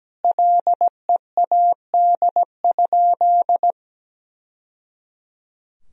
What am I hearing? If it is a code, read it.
Morse code, heard as a keyed tone, about 17 wpm, 700 Hz. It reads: LEAD?
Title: Sentences from Top 1000 Words - Review Part 2 - 17wpm